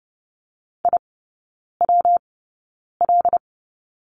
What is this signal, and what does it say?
Morse code, heard as a keyed tone, about 30 wpm, 700 Hz. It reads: I W L
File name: Lesson 19e - Letter w - Three Characters 30wpm